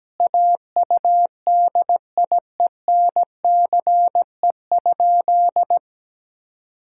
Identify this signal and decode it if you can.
Morse code, heard as a keyed tone, about 17 wpm, 700 Hz. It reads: AUDIENCE?
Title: Sentences from Top 1000 Words - Review Part 5 - 17wpm